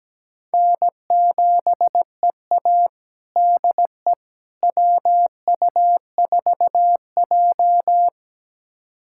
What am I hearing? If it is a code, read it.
Morse code, heard as a keyed tone, about 17 wpm, 700 Hz. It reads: N7EA DE WU4J
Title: USA Callsign DE Callsign - 4 Characters - 17wpm